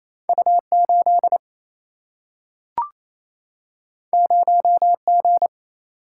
Morse code, heard as a keyed tone, about 28 wpm, 700 Hz. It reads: U8 0G
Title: Single-Letter Single-Number 28wpm